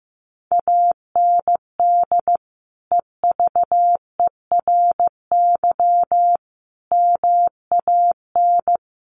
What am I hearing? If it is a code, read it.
Morse code, heard as a keyed tone, about 15 wpm, 700 Hz. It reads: AND EVERY MAN
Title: Sentences from Top 1500 Words 15wpm